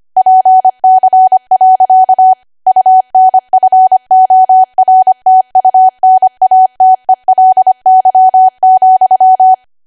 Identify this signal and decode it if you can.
Morse code, heard as a keyed tone, about 25 wpm, 750 Hz. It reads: PC. UNFORTUNATELY,
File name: Archive / 25 WPM / 25WPM